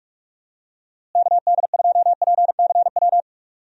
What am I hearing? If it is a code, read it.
Morse code, heard as a keyed tone, about 46 wpm, 700 Hz. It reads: KD2PCW
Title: USA Call Signs - 6 Characters - Repeated 2x - 45wpm